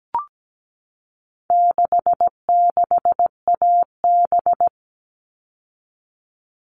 Morse code, heard as a keyed tone, about 17 wpm, 700 Hz. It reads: 66AB
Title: All Cut Numbers - 4-Digit 17wpm